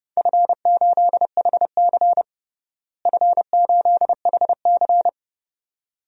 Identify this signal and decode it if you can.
Morse code, heard as a keyed tone, about 30 wpm, 700 Hz. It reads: F8HC F8HC
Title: DXCC-100 Call Signs - 4 Characters - Repeated 2x - 30wpm